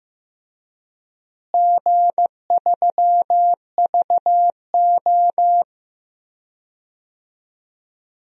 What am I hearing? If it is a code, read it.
Morse code, heard as a keyed tone, about 15 wpm, 700 Hz. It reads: G3VO
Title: DXCC-100 Call Signs - 4 Characters - Repeated 2x - 15wpm